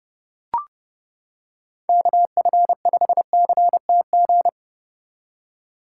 Morse code, heard as a keyed tone, about 30 wpm, 700 Hz. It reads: KF5CTG